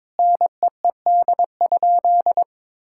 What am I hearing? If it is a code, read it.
Morse code, heard as a keyed tone, about 22 wpm, 700 Hz. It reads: NEED?
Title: Sentences from Top 1000 Words - Review Part 5 - 22wpm